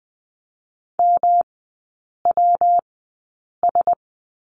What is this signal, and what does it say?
Morse code, heard as a keyed tone, about 20 wpm, spaced as for 8 wpm, 700 Hz. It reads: MWS
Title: 114 Lesson 19e - Letter w - Three Characters 20wpm